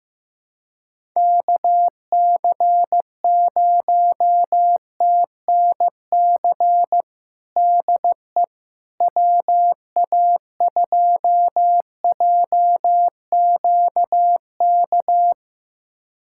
Morse code, heard as a keyed tone, about 15 wpm, 700 Hz. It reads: KC0TNC DE WA2JQK
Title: USA Callsign DE Callsign - 6 Characters - 15wpm